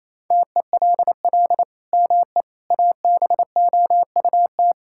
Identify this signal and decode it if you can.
Morse code, heard as a keyed tone, about 28 wpm, 700 Hz. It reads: TELL ME ABOUT
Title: Sentences from Top 1000 Words - Review Part 5 - 28wpm